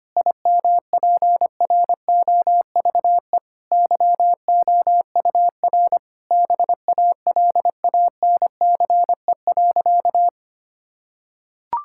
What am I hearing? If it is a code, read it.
Morse code, heard as a keyed tone, about 25 wpm, 700 Hz. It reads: IMPROVE YOUR BALANCE.
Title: Sentences from Top 1000 Words - Review Part 6 - 25wpm